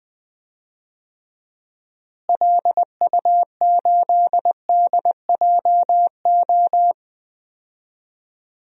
Morse code, heard as a keyed tone, about 20 wpm, 700 Hz. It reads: LU8DJO
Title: DXCC-100 Call Signs - 6 Characters - Repeated 3x - 20wpm